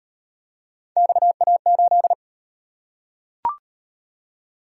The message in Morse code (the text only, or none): XA8